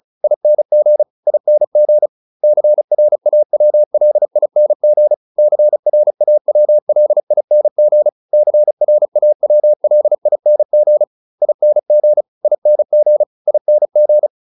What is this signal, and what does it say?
Morse code, heard as a keyed tone, about 34 wpm, 600 Hz. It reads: ING ING CRAWLING CRAWLING CRAWLING ING ING ING